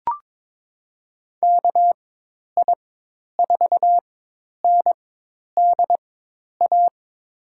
Morse code, heard as a keyed tone, about 22 wpm, spaced as for 10 wpm, 700 Hz. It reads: KI4NDA
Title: USA Call Signs - 6 Characters - 4x Character Spacing - 22wpm